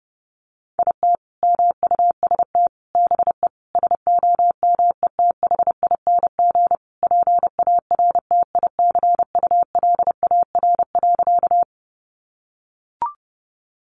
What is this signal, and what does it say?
Morse code, heard as a keyed tone, about 30 wpm, 700 Hz. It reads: IT MUST BE SOMETHING PARTICULAR.
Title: Sentences from Top 600 Words 30wpm